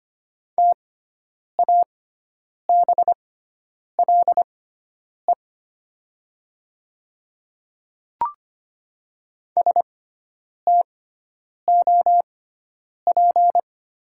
Morse code, heard as a keyed tone, about 25 wpm, spaced as for 9 wpm, 700 Hz. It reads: TABLE STOP
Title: 6x Character Spacing - Top 500 Words - 25wpm